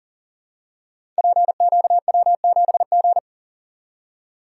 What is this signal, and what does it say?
Morse code, heard as a keyed tone, about 42 wpm, 700 Hz. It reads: PQW7G
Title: Code Groups - 5 Letter:Numbers - 40wpm